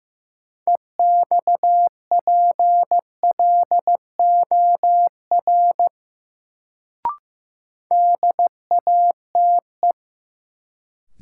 Morse code, heard as a keyed tone, about 15 wpm, 700 Hz. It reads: EXPLOR DATE
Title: Sentences from Top 1100 Words - New Word Drill Alt - 15wpm